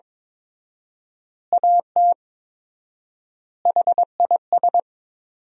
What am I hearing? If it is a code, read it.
Morse code, heard as a keyed tone, about 22 wpm, 700 Hz. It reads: AT HIS
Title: 4x Word-Spacing - Sentences from Top 300 Words - 22wpm